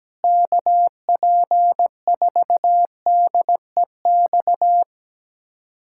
Morse code, heard as a keyed tone, about 17 wpm, 700 Hz. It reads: KP4DEX
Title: USA Callsign DE Callsign - 6 Characters - 17wpm